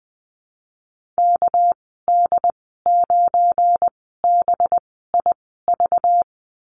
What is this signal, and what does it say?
Morse code, heard as a keyed tone, about 19 wpm, spaced as for 14 wpm, 700 Hz. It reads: KD9BIV
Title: 231 Lesson 37g - Letter x - Call Signs 20wpm